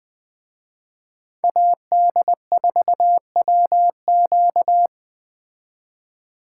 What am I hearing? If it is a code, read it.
Morse code, heard as a keyed tone, about 20 wpm, 700 Hz. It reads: AD4WQ
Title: Speed Racing - USA Call Signs 17wpm